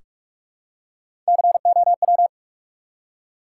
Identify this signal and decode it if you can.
Morse code, heard as a keyed tone, about 45 wpm, 700 Hz. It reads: COW
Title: Three Letter Words - 45wpm